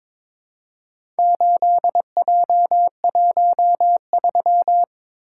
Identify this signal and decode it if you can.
Morse code, heard as a keyed tone, about 22 wpm, 700 Hz. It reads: 8J13
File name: practice / 22wpm / Four Letter-Number Combinations - 22wpm